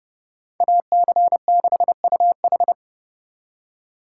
Morse code, heard as a keyed tone, about 35 wpm, 700 Hz. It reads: AC6UH
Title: Speed Racing - USA Call Signs 30wpm